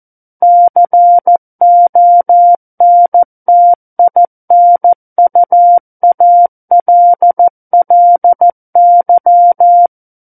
Morse code, heard as a keyed tone, about 14 wpm, 700 Hz. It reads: CONTINUALLY
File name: Book One / 05-Chapter 5 - 14 WPM